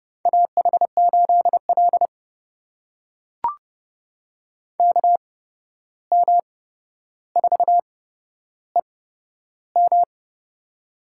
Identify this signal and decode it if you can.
Morse code, heard as a keyed tone, about 30 wpm, spaced as for 12 wpm, 700 Hz. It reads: AH8L KM4EM
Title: USA Call Signs - 8x Character Spacing - 30wpm